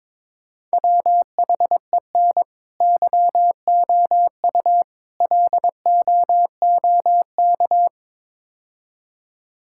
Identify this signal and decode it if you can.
Morse code, heard as a keyed tone, about 22 wpm, 700 Hz. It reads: WHEN YOU LOOK
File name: practice / 22wpm / Build A Phrase - 5-Word Phrases - 22wpm